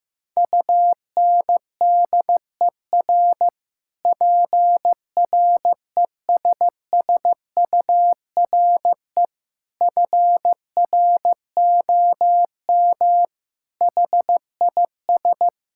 Morse code, heard as a keyed tone, about 15 wpm, 700 Hz. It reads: UNDER PRESSURE FROM HIS
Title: Sentences from Top 1000 Words - Review Part 6 - 15wpm